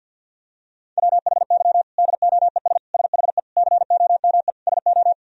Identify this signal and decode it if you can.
Morse code, heard as a keyed tone, about 47 wpm, 700 Hz. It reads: WHY DOES SHE COME SO
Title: Sentences from Top 1000 Words - Review Part 1 - 50wpm